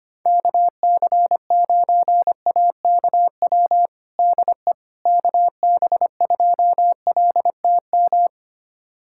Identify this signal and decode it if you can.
Morse code, heard as a keyed tone, about 25 wpm, 700 Hz. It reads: KC9AKW DE KB2LTM
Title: USA Callsign DE Callsign 25wpm